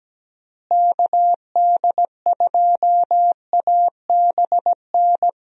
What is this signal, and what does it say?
Morse code, heard as a keyed tone, about 17 wpm, 700 Hz. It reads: KD2ABN
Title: USA Call Signs - 6 Characters - Repeated 2x - 17wpm